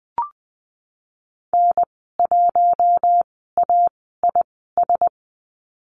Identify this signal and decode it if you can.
Morse code, heard as a keyed tone, about 20 wpm, spaced as for 15 wpm, 700 Hz. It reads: N1AIS
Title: 069 Lesson 12g - Letter d - Call Signs 20wpm